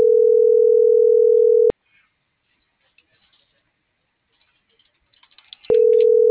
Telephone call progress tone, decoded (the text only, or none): ringback tone